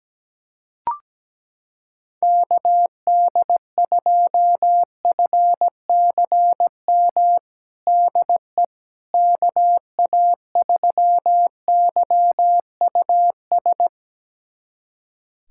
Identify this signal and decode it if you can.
Morse code, heard as a keyed tone, about 17 wpm, 700 Hz. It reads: KD2FCM DE KA3YUS
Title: USA Callsign DE Callsign - 6 Characters - 17wpm